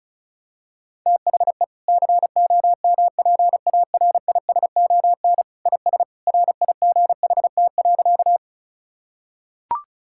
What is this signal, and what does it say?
Morse code, heard as a keyed tone, about 36 wpm, 700 Hz. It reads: THE COMPARISON IS RIGHT.
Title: Sentences from Top 1600 Words 35wpm